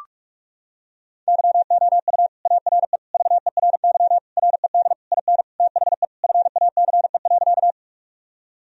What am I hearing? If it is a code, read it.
Morse code, heard as a keyed tone, about 42 wpm, 700 Hz. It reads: YOU ARE VERY RED IN THE FACE.
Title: Sentences from Top 1000 Words - Review Part 2 - 45wpm